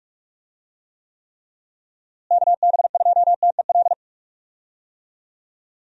Morse code, heard as a keyed tone, about 41 wpm, 700 Hz. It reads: KB2TEL
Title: USA Call Signs - 6 Characters - Repeated 3x - 45wpm